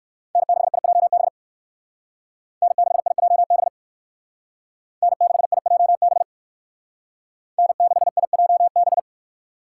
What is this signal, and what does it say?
Morse code, heard as a keyed tone, about 58 wpm, 690 Hz. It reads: N6IJB N6IJB N6IJB N6IJB